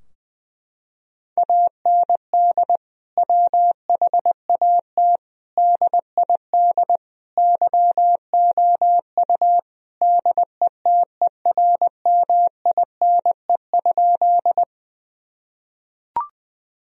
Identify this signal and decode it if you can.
Morse code, heard as a keyed tone, about 21 wpm, 700 Hz. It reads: AND WHAT DID YOU DETERMINE?
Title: Sentences from Top 1000 Words - Review Part 4 - 20wpm